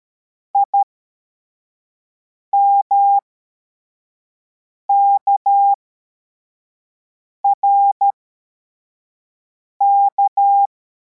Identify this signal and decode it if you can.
Morse code, heard as a keyed tone, about 13 wpm, spaced as for 4 wpm, 800 Hz. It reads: IMKRK
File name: formation / cw / Text33